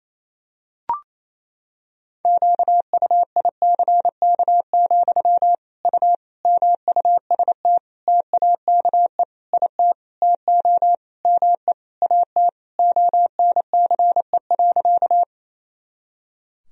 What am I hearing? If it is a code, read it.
Morse code, heard as a keyed tone, about 27 wpm, 700 Hz. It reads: QUICK, U MUST TAKE IT TO ME AT ONCE.
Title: Sentences from Top 500 Words and Abbreviations 28wpm